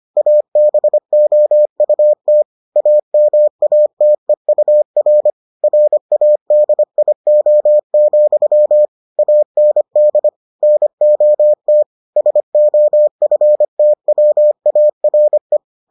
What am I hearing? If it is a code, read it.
Morse code, heard as a keyed tone, about 25 wpm, 600 Hz. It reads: ABOUT AMATEUR RADIO, AND NOT SOFTWARE